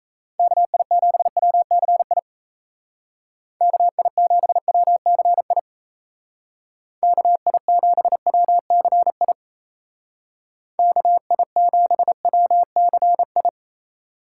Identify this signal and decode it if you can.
Morse code, heard as a keyed tone, about 36 wpm, 700 Hz. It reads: KI7WCI KI7WCI KI7WCI KI7WCI